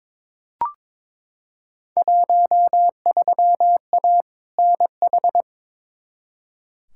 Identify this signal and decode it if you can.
Morse code, heard as a keyed tone, about 21 wpm, 700 Hz. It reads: 13A NH
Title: Field Day Exchanges Encore - 22wpm